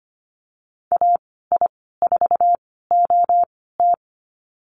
Morse code, heard as a keyed tone, about 25 wpm, spaced as for 16 wpm, 700 Hz. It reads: AI4OT
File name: Lesson 15g - Number 5 - Call Signs 25wpm